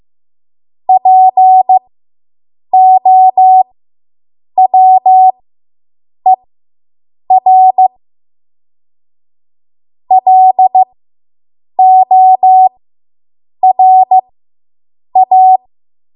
Morse code, heard as a keyed tone, about 15 wpm, spaced as for 7 wpm, 750 Hz. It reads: POWER LORA